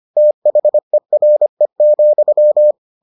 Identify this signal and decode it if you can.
Morse code, heard as a keyed tone, about 25 wpm, 600 Hz. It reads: THERE,